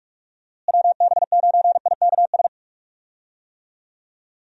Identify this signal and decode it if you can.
Morse code, heard as a keyed tone, about 46 wpm, 700 Hz. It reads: WB9IKS
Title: USA Call Signs - 6 Characters - Repeated 2x - 45wpm